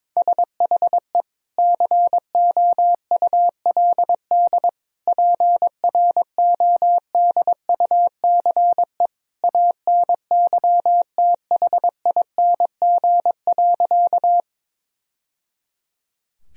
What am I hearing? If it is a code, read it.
Morse code, heard as a keyed tone, about 22 wpm, 700 Hz. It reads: SHE COULD PRODUCE ANYTHING.
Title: Sentences from Top 1000 Words - Review Part 3 - 22wpm